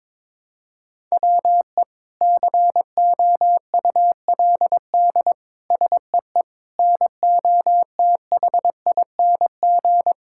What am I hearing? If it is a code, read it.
Morse code, heard as a keyed tone, about 22 wpm, 700 Hz. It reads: WE COULD SEE NOTHING
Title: Sentences from Top 1000 Words - Review Part 5 - 22wpm